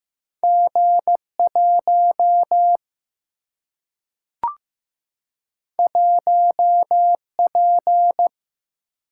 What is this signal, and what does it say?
Morse code, heard as a keyed tone, about 15 wpm, 700 Hz. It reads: G 1 1 P
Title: Single-Letter Single-Number 15wpm